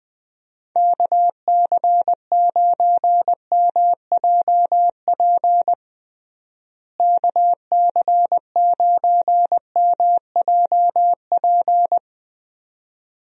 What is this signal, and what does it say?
Morse code, heard as a keyed tone, about 20 wpm, 700 Hz. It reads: KC9MJP KC9MJP
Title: USA Call Signs - 6 Characters - Repeated 2x - 20wpm